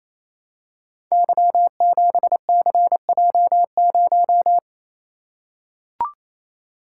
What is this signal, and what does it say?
Morse code, heard as a keyed tone, about 29 wpm, 700 Hz. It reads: Y7CJ0
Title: Code Groups - 5 Letter:Numbers - 28wpm